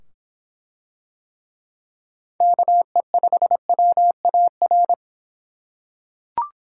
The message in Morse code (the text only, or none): KE5WAR